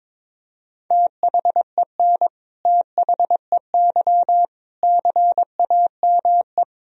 Morse code, heard as a keyed tone, about 22 wpm, 700 Hz. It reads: THEN THEY CAME
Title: Sentences from Top 1000 Words - Review Part 1 - 22wpm